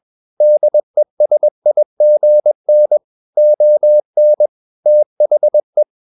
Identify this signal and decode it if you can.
Morse code, heard as a keyed tone, about 21 wpm, 600 Hz. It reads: DESIGN ON THE
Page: CW med Gnister